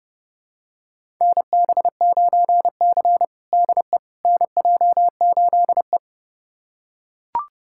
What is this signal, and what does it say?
Morse code, heard as a keyed tone, about 30 wpm, 700 Hz. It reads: NB9C DE NJ8E
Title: USA Callsign DE Callsign - 4 Characters - 30wpm